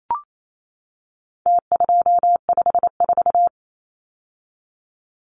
Morse code, heard as a keyed tone, about 28 wpm, 700 Hz. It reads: T254
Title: Numbers - Contest Style - 4-Digit 28wpm